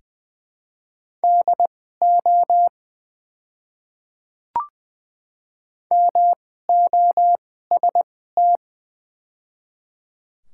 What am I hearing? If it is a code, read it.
Morse code, heard as a keyed tone, about 20 wpm, spaced as for 15 wpm, 700 Hz. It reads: DO MOST
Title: 2x Character Spacing - Top 200 Words - 20wpm